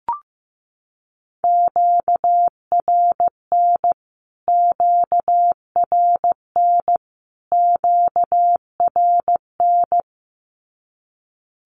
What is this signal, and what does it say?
Morse code, heard as a keyed tone, about 15 wpm, 700 Hz. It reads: QRN QRN QRN
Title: Q-Codes Repeated 3x 15wpm